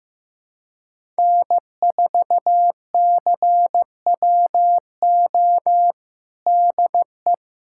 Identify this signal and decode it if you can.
Morse code, heard as a keyed tone, about 15 wpm, 700 Hz. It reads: N4CWO DE
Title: USA Callsign DE Callsign - 5 Characters - 15wpm